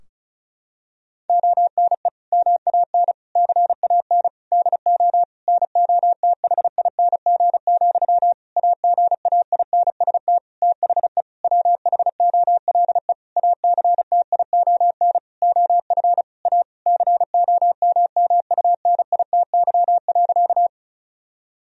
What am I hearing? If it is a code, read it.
Morse code, heard as a keyed tone, about 34 wpm, 700 Hz. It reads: ONE MAN CAN DO NOTHING, AGAINST THE WHOLE ACTION OF A COMMUNITY.
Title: Sentences from Top 1000 Words - Review Part 4 - 35wpm